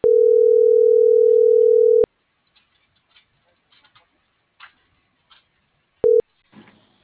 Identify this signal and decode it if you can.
Ringback tone (North America), 440 + 480 Hz.